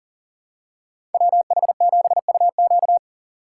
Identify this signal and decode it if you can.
Morse code, heard as a keyed tone, about 40 wpm, 700 Hz. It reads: WH7UQ
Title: USA Call Signs - 5 Characters - 6x Character Spacing - 40wpm